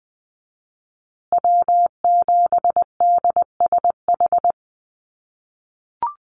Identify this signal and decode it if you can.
Morse code, heard as a keyed tone, about 20 wpm, 700 Hz. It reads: W7DSH
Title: USA Call Signs Repeated 3x 20wpm